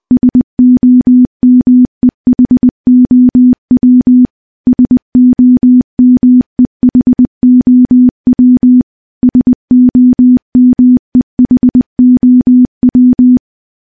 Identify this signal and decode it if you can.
Morse code, heard as a keyed tone, about 20 wpm, 250 Hz. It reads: SOMEHOW SOMEHOW SOMEHOW
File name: morse-training-20WPM